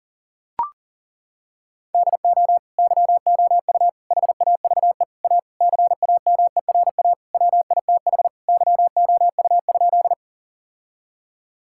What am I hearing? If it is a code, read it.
Morse code, heard as a keyed tone, about 37 wpm, 700 Hz. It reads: DO YOU HAVE A CAMERA WITH YOU?
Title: Sentences from Top 1000 Words - Review Part 5 - 40wpm